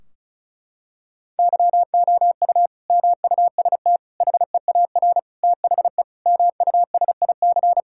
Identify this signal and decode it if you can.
Morse code, heard as a keyed tone, about 36 wpm, 700 Hz. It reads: YOU MUST HEAR THE MUSIC
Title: Sentences from Top 1600 Words 35wpm